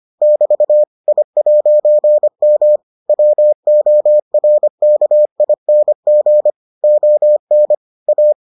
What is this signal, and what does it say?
Morse code, heard as a keyed tone, about 25 wpm, 600 Hz. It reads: <BT> I'M WORKING ON A